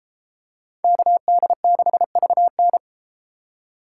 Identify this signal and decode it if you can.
Morse code, heard as a keyed tone, about 33 wpm, 700 Hz. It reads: KD6VN